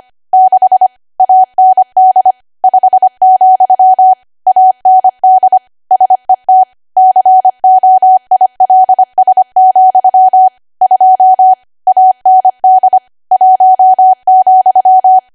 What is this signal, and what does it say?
Morse code, heard as a keyed tone, about 25 wpm, 750 Hz. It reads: 6 AND 5, AND SET COILS, 2 AND 1,
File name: cwa / W1AW-Level3 / 25WPM